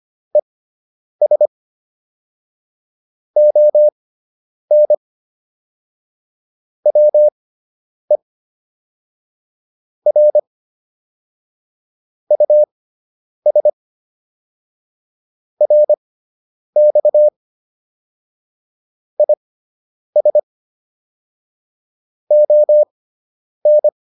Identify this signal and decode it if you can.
Morse code, heard as a keyed tone, about 25 wpm, spaced as for 9 wpm, 600 Hz. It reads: ES ON WE R US RX IS ON